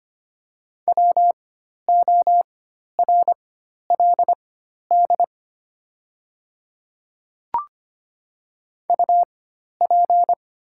Morse code, heard as a keyed tone, about 25 wpm, spaced as for 12 wpm, 700 Hz. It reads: WORLD UP
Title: 4x Character Spacing - Top 200 Words - 25wpm